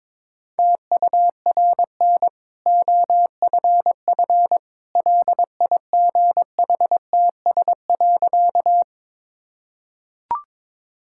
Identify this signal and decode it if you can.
Morse code, heard as a keyed tone, about 22 wpm, 690 Hz. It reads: TURN OFF LIGHTS.
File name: practice / 22wpm / Easy Sentences with Hints - 22wpm